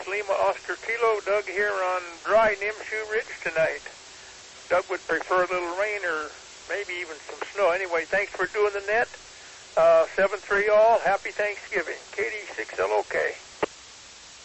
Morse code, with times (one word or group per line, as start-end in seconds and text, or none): none